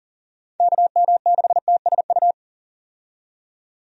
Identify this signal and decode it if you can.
Morse code, heard as a keyed tone, about 40 wpm, 700 Hz. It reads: KM6TSU
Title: USA Call Signs - 6 Characters - 40wpm